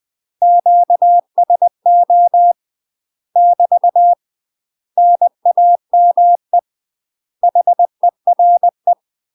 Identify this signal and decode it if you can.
Morse code, heard as a keyed tone, about 20 wpm, 700 Hz. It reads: QSO <BT> NAME HERE
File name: UK MASTER QSOs TRAINING / 20WPM / UK MASTER QSOs 0031